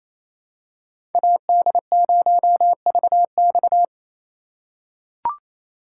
Morse code, heard as a keyed tone, about 28 wpm, 700 Hz. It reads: AD0VX
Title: USA Call Signs - 5 Characters - 2x Character Spacing - 28wpm